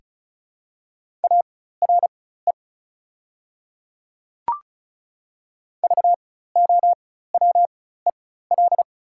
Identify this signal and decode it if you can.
Morse code, heard as a keyed tone, about 36 wpm, spaced as for 16 wpm, 700 Hz. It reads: ARE VOWEL
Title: 4x Character Spacing - Top 500 Words - 35wpm